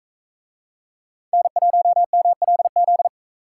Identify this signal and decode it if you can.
Morse code, heard as a keyed tone, about 44 wpm, 700 Hz. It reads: N1MLZ